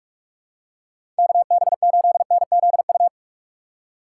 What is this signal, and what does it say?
Morse code, heard as a keyed tone, about 46 wpm, 700 Hz. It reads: KB8NZU
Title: USA Call Signs - 6 Characters - 6x Character Spacing - 45wpm